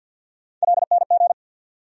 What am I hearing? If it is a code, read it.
Morse code, heard as a keyed tone, about 52 wpm, 700 Hz. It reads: LNG
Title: USA Call Sign Postfixes - 3 Letters 50wpm